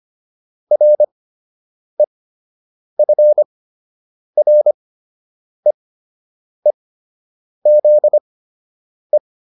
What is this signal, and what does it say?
Morse code, heard as a keyed tone, about 25 wpm, spaced as for 8 wpm, 600 Hz. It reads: REFREEZE